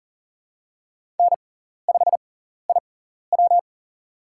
Morse code, heard as a keyed tone, about 40 wpm, spaced as for 14 wpm, 700 Hz. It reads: N5IW
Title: USA Call Signs - 4 Characters - 6x Character Spacing - 40wpm